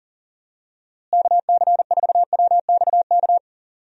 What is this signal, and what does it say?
Morse code, heard as a keyed tone, about 40 wpm, 700 Hz. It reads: KC4WXK
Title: USA Call Signs - 6 Characters - Repeated 3x - 40wpm